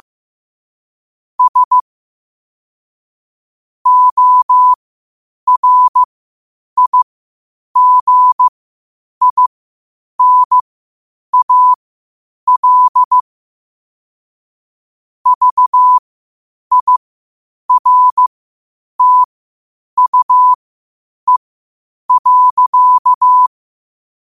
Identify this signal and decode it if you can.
Morse code, heard as a keyed tone, about 15 wpm, spaced as for 8 wpm, 1000 Hz. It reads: S ORIGINAL VIRTUE.
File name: qotdfiles / 8wpm